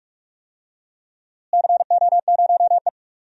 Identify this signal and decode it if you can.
Morse code, heard as a keyed tone, about 42 wpm, 700 Hz. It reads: CO0E
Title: DXCC-100 Call Signs - 4 Characters - Repeated 2x - 45wpm